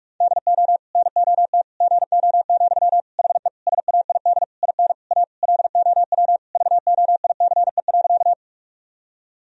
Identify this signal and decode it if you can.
Morse code, heard as a keyed tone, about 46 wpm, 700 Hz. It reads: DO NOT GO, HE SAID IN A LOW VOICE.
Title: Sentences from Top 1000 Words - Review Part 1 - 45wpm